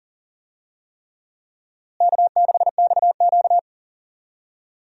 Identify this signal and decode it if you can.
Morse code, heard as a keyed tone, about 40 wpm, 700 Hz. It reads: K6XQ
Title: USA Call Signs - 4 Characters - Repeated 3x - 40wpm